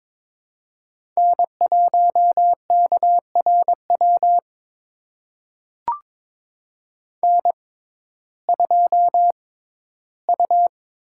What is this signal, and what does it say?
Morse code, heard as a keyed tone, about 22 wpm, spaced as for 14 wpm, 700 Hz. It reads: N1KRW N2U
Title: USA Call Signs - 5 Characters - 6x Character Spacing - 22wpm